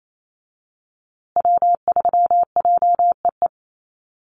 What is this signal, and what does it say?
Morse code, heard as a keyed tone, about 28 wpm, 700 Hz. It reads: W3JEE
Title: USA Call Signs Repeated 2x 28wpm